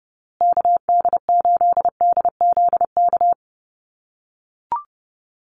Morse code, heard as a keyed tone, about 30 wpm, 700 Hz. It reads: KD8DZK
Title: USA Call Signs Repeated 2x 30wpm